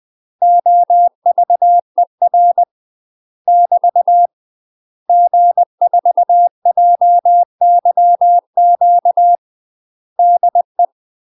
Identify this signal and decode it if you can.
Morse code, heard as a keyed tone, about 20 wpm, 700 Hz. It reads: OVER <BT> G4JYQ DE
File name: UK MASTER QSOs TRAINING / 20WPM / UK MASTER QSOs 0035